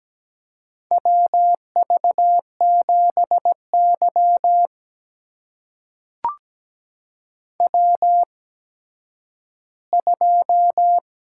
Morse code, heard as a keyed tone, about 17 wpm, spaced as for 10 wpm, 700 Hz. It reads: WV7Y W2
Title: USA Call Signs - 4 Characters - 8x Character Spacing - 17wpm